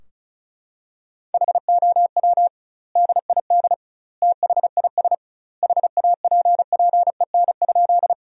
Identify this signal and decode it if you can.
Morse code, heard as a keyed tone, about 34 wpm, 700 Hz. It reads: HOW DID THIS HAPPEN?
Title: 2x Word-Spacing - Sentences from Top 400 Words - 35wpm